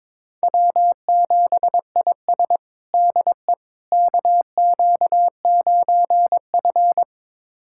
Morse code, heard as a keyed tone, about 22 wpm, 700 Hz. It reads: W7IS DE KQ9F
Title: USA Callsign DE Callsign - 4 Characters - 22wpm